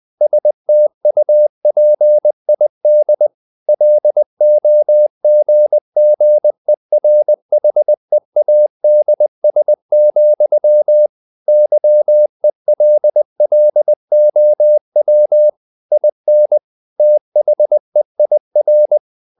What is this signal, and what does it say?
Morse code, heard as a keyed tone, about 20 wpm, 600 Hz. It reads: STUPID LOGGERHEADS, YELLOW IN THEIR
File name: the old man and the sea / the old man and the sea part 1 20wpm